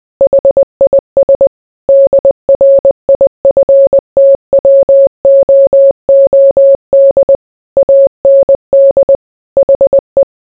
Morse code, heard as a keyed tone, about 20 wpm, 550 Hz. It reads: HIS DRIFTWOOD AND HE